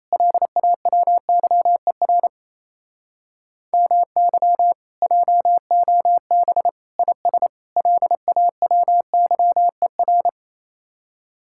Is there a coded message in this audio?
Yes: Morse code from 0.1 to 10.3 s, heard as a keyed tone, about 35 wpm, 700 Hz. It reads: LAWYER MY JOB IS LAWYER